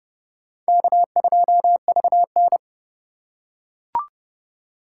Morse code, heard as a keyed tone, about 30 wpm, 700 Hz. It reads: K2VN
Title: USA Call Signs - 2x Character Spacing - 30wpm